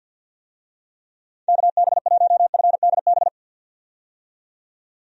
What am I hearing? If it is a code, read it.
Morse code, heard as a keyed tone, about 50 wpm, 700 Hz. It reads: KB1FDB
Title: USA Call Signs - 6 Characters - 50wpm